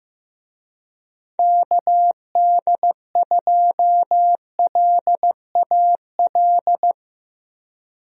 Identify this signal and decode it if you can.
Morse code, heard as a keyed tone, about 15 wpm, 700 Hz. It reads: KD2LAL
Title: USA Call Signs - 6 Characters - Repeated 2x - 15wpm